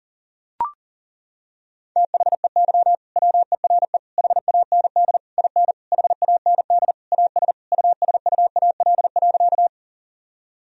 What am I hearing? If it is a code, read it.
Morse code, heard as a keyed tone, about 37 wpm, 700 Hz. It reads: THEY WERE HAND IN HAND AS USUAL.
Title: Sentences from Top 1000 Words - Review Part 2 - 40wpm